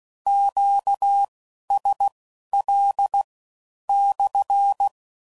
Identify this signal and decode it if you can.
Morse code, heard as a keyed tone, about 16 wpm, spaced as for 11 wpm, 800 Hz. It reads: QSL /
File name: CW Code Groups @15WPM / Q Calls / Q Calls SPSL